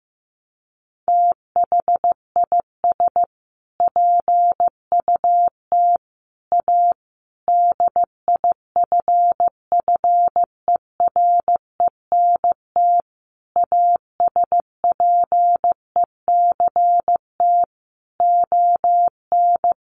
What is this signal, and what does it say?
Morse code, heard as a keyed tone, about 14 wpm, 700 Hz. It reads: THIS PUT A DIFFERENT ASPECT ON
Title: Sentences from Top 1300 Words 15wpm